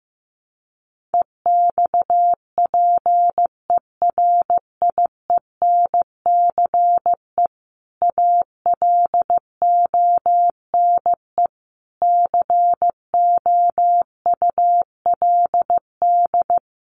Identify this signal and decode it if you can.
Morse code, heard as a keyed tone, about 15 wpm, 700 Hz. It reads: EXPERIENCE ALONE COULD
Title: Sentences from Top 1000 Words 15wpm